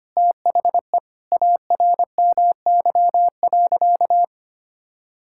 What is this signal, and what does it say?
Morse code, heard as a keyed tone, about 25 wpm, 700 Hz. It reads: THE ARMY.
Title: Sentences from Top 1000 Words - Review Part 6 - 25wpm